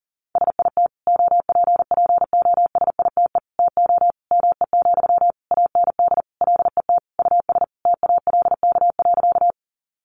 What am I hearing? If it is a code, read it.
Morse code, heard as a keyed tone, about 40 wpm, 700 Hz. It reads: SIT OPPOSITE TO ME, AND LET US TALK.
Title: Sentences from Top 1300 Words 40wpm